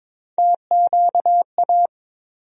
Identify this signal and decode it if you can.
Morse code, heard as a keyed tone, about 22 wpm, 700 Hz. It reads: TQA
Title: USA Call Sign Postfixes - 3 Letters 22wpm